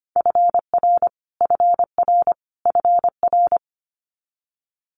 Morse code, heard as a keyed tone, about 25 wpm, 700 Hz. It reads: FR FR FR
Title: Multi-letter Phonemes Repeated 3x 25wpm